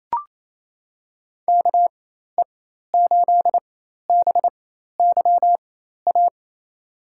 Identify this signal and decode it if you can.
Morse code, heard as a keyed tone, about 28 wpm, spaced as for 13 wpm, 700 Hz. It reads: KE8BYA
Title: USA Call Signs - 6 Characters - 4x Character Spacing - 28wpm